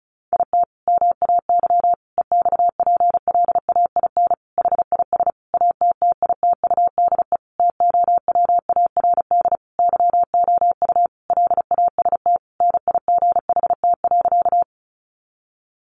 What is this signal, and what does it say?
Morse code, heard as a keyed tone, about 34 wpm, 700 Hz. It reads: IT MAY EXPLAIN HIS ATTITUDE TOWARD YOU LAST NIGHT.
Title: Sentences from Top 1300 Words 35wpm